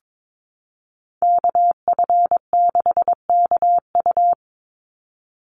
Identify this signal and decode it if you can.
Morse code, heard as a keyed tone, about 22 wpm, 700 Hz. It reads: KF6KU
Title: USA Call Signs 22wpm